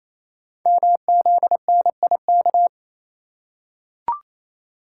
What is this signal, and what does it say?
Morse code, heard as a keyed tone, about 27 wpm, 700 Hz. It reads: MZNIK
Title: Code Groups - 5 Letters - 28wpm